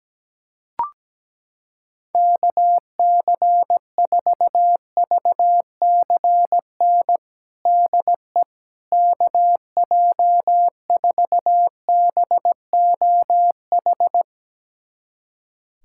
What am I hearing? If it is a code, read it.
Morse code, heard as a keyed tone, about 17 wpm, 700 Hz. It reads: KC4VCN DE KJ4BOH
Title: USA Callsign DE Callsign - 6 Characters - 17wpm